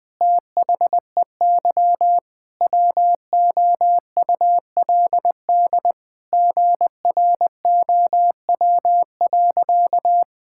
Morse code, heard as a keyed tone, about 21 wpm, 700 Hz. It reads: THEY WOULD GROW.
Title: Sentences with 9-Word Phrases - Spoken Beginning - 20wpm